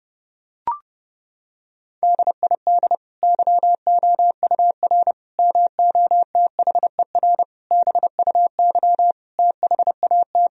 Morse code, heard as a keyed tone, about 30 wpm, 700 Hz. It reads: DID YOUR MOTHER BUY THAT
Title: Sentences from Top 1000 Words - Review Part 3 - 30wpm